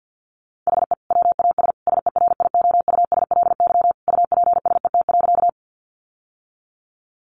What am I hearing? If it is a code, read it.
Morse code, heard as a keyed tone, about 50 wpm, 700 Hz. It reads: HE WAS SERIOUSLY UPSET.
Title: Sentences from Top 1200 Words 50wpm